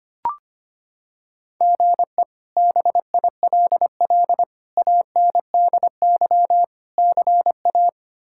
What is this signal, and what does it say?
Morse code, heard as a keyed tone, about 25 wpm, 700 Hz. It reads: GE BILL ANDY CA
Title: K1USN SST QSO Practice - 25wpm